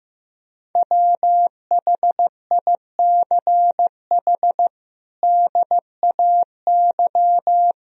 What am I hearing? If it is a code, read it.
Morse code, heard as a keyed tone, about 15 wpm, 700 Hz. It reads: WHICH DAY
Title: Sentences from Top 1000 Words - Review Part 3 - 15wpm